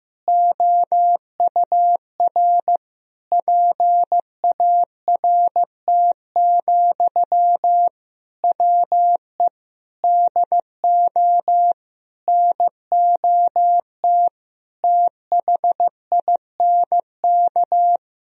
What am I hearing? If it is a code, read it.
Morse code, heard as a keyed tone, about 16 wpm, 700 Hz. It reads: OUR PART, WE DO NOT THINK
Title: Sentences from Top 1000 Words - Review Part 1 - 15wpm